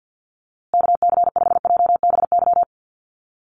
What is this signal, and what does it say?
Morse code, heard as a keyed tone, about 50 wpm, 700 Hz. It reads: KC5JBY
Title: USA Call Signs Repeated 3x 50wpm